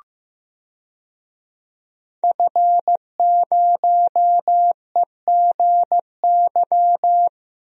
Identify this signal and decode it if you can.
Morse code, heard as a keyed tone, about 15 wpm, 700 Hz. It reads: F0EGY
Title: DXCC-100 Call Signs - 5 Characters - Repeated 2x - 15wpm